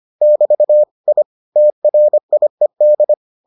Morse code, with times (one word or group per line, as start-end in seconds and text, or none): 0.2-0.8 <BT>
1.1-1.2 I
1.6-3.1 TRIED